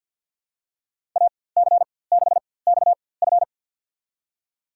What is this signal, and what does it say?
Morse code, heard as a keyed tone, about 53 wpm, spaced as for 23 wpm, 700 Hz. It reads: AC6XF
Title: USA Call Signs - 4x Character Spacing - 50wpm